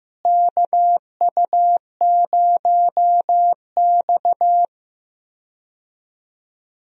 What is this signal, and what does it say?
Morse code, heard as a keyed tone, about 15 wpm, 700 Hz. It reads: KU0X
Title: USA Call Signs - 4 Characters - Repeated 2x - 15wpm